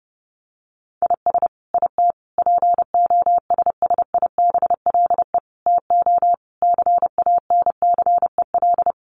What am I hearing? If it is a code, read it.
Morse code, heard as a keyed tone, about 30 wpm, 700 Hz. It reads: IS IT POSSIBLE TO CANCEL
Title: Sentences from Top 1200 Words 30wpm